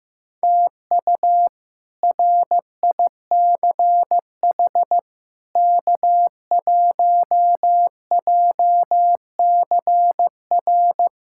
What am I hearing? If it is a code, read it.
Morse code, heard as a keyed tone, about 15 wpm, 700 Hz. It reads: TU RICH K1JCR
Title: K1USN SST QSO Practice - 15wpm